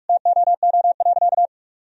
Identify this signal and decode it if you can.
Morse code, heard as a keyed tone, about 46 wpm, 700 Hz. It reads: TOO.
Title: Sentences from Top 1000 Words - Review Part 3 - 45wpm